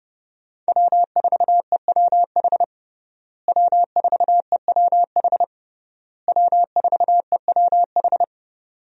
Morse code, heard as a keyed tone, about 29 wpm, 700 Hz. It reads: W4EWH W4EWH W4EWH
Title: USA Call Signs - 5 Characters - Repeated 3x - 30wpm